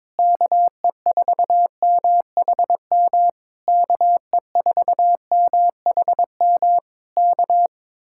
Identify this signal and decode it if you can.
Morse code, heard as a keyed tone, about 21 wpm, 700 Hz. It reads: KE4MHM KE4MHM K
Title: Speed-Racing Common QSO Phrases Encore - 22wpm